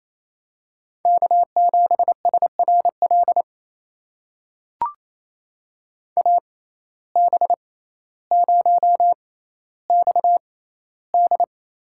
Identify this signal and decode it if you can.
Morse code, heard as a keyed tone, about 28 wpm, spaced as for 14 wpm, 700 Hz. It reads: K7SRL AB0XD
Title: USA Call Signs - 5 Characters - 6x Character Spacing - 28wpm